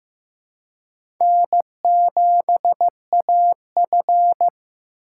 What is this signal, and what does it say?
Morse code, heard as a keyed tone, about 15 wpm, 700 Hz. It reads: N7AF
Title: USA Call Signs - 4 Characters - 3x Character Spacing - 15wpm